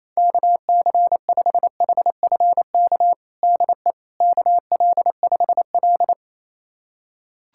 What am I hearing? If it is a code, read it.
Morse code, heard as a keyed tone, about 28 wpm, 700 Hz. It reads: KC5HFK DE KL5L
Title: USA Callsign DE Callsign 28wpm